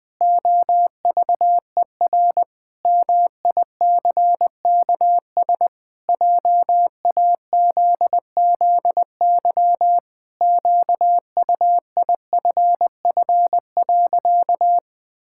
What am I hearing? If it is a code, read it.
Morse code, heard as a keyed tone, about 21 wpm, 700 Hz. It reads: OVER MICKS JAZZY QUIFF.
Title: Pangram Challenge v2 - 20wpm